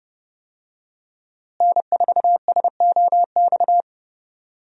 Morse code, heard as a keyed tone, about 32 wpm, 700 Hz. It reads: N4SOX
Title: USA Call Signs - 5 Characters - Repeated 3x - 30wpm